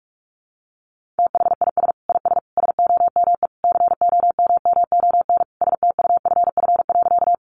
Morse code, heard as a keyed tone, about 44 wpm, 700 Hz. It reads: THIS IS SOME COMMON STUFF.
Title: Sentences from Top 1400 Words 45wpm